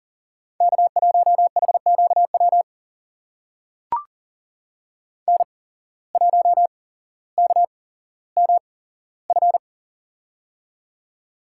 Morse code, heard as a keyed tone, about 42 wpm, spaced as for 17 wpm, 700 Hz. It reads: K1HQW N1KMF
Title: USA Call Signs - 5 Characters - 8x Character Spacing - 40wpm